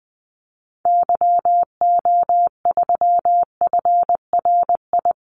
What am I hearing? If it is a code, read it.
Morse code, heard as a keyed tone, about 20 wpm, 700 Hz. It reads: YO3FRI
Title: Call Signs Repeated 3x - 20wpm